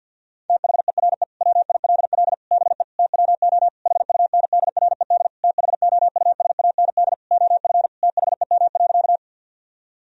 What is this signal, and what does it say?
Morse code, heard as a keyed tone, about 50 wpm, 700 Hz. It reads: THERE WILL BE TWO HUNDRED THOUSAND OF THEM.
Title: Sentences from Top 1000 Words - Review Part 3 - 50wpm